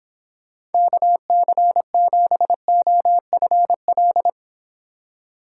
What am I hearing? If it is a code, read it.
Morse code, heard as a keyed tone, about 29 wpm, 700 Hz. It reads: KC7OFL